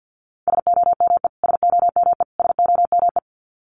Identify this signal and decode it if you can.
Morse code, heard as a keyed tone, about 50 wpm, 700 Hz. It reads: SOME SOME SOME
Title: Top 500 Words Repeated 50wpm